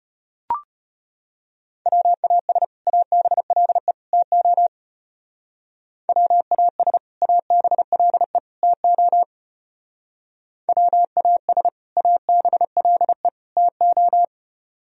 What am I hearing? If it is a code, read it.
Morse code, heard as a keyed tone, about 31 wpm, 700 Hz. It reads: WAS ABLE TO WAS ABLE TO WAS ABLE TO
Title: Speed-Racing - Sets of 3 Words 25wpm